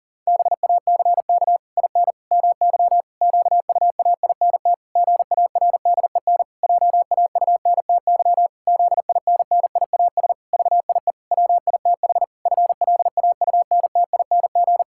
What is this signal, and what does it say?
Morse code, heard as a keyed tone, about 40 wpm, 700 Hz. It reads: BACK IN MY QUAINT GARDEN JAUNTY ZINNIAS VIE WITH FLAUNTING
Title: Pangram Challenge v2 - 40wpm